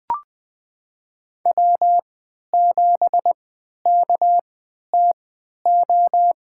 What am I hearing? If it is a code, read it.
Morse code, heard as a keyed tone, about 20 wpm, spaced as for 11 wpm, 700 Hz. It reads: W7KTO
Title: USA Call Signs - 3x Character Spacing - 20wpm